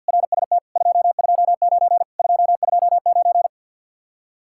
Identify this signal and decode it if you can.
Morse code, heard as a keyed tone, about 64 wpm, 700 Hz. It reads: RST 229 229